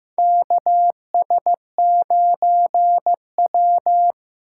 Morse code, heard as a keyed tone, about 15 wpm, 700 Hz. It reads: KS9W
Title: K1USN SST QSO Practice - 15wpm